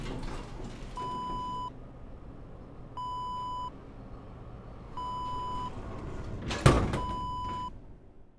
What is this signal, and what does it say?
There is no code.